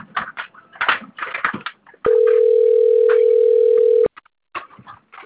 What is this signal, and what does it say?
Ringback tone (North America), 440 + 480 Hz.